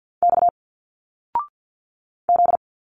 Morse code, heard as a keyed tone, about 51 wpm, 700 Hz. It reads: X Z
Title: Single Letters 50wpm